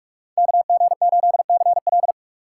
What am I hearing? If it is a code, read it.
Morse code, heard as a keyed tone, about 46 wpm, 700 Hz. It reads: KG8CL
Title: Speed Racing - USA 2x2 Call Signs 45wpm